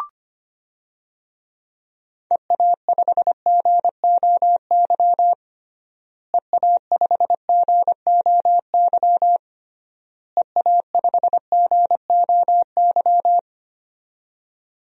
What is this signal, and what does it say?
Morse code, heard as a keyed tone, about 25 wpm, 700 Hz. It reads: EA5GOY EA5GOY EA5GOY
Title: DXCC-100 Call Signs - 6 Characters - Repeated 3x - 25wpm